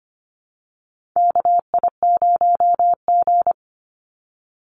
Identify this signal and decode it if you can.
Morse code, heard as a keyed tone, about 25 wpm, 700 Hz. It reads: KI0G
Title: NA SOTA Chaser Callsigns 25wpm